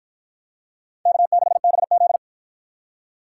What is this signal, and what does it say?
Morse code, heard as a keyed tone, about 53 wpm, 700 Hz. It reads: K6BZ